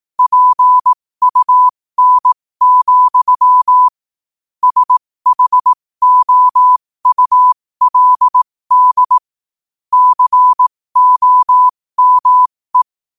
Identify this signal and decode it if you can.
Morse code, heard as a keyed tone, about 18 wpm, spaced as for 15 wpm, 1000 Hz. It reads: PUN, SHOULD COME
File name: qotdfiles / 15wpm